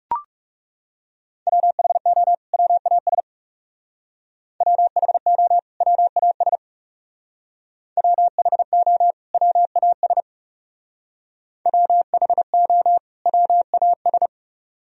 Morse code, heard as a keyed tone, about 36 wpm, 700 Hz. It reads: WHO WAS WHO WAS WHO WAS WHO WAS
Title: Speed-Racing - Sets of 2 Words 30wpm